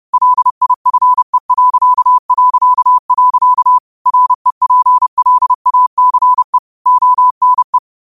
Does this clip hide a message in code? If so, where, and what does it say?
Yes: Morse code from 0.1 to 7.8 s, heard as a keyed tone, about 30 wpm, 1000 Hz. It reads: LIFE... REPLACE ONE